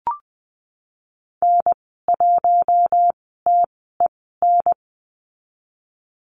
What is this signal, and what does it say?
Morse code, heard as a keyed tone, about 19 wpm, spaced as for 14 wpm, 700 Hz. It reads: N1TEN
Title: 048 Lesson 9g - Number 4 - Call Signs 20wpm